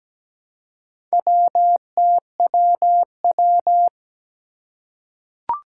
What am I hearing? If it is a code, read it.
Morse code, heard as a keyed tone, about 17 wpm, 700 Hz. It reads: WTWW
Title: Four Challenging Letter-Number Combinations 17wpm